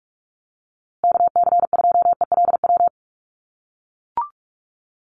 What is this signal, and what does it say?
Morse code, heard as a keyed tone, about 44 wpm, 700 Hz. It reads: KC2ELW
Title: USA Call Signs Repeated 2x 45wpm